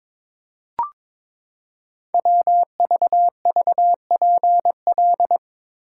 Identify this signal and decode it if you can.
Morse code, heard as a keyed tone, about 22 wpm, 700 Hz. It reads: WVVPL
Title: Code Groups - 5 Letter:Numbers - 22wpm